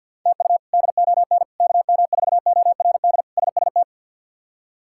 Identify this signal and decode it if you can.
Morse code, heard as a keyed tone, about 52 wpm, 700 Hz. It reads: TU DON KM4ORD SST
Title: K1USN SST QSO Practice - 50wpm